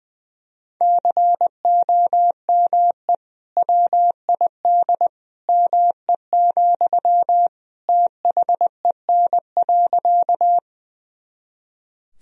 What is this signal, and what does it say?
Morse code, heard as a keyed tone, about 20 wpm, 700 Hz. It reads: COME WID ME, THEN.
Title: Sentences from Top 200 Words and Abbreviations 20wpm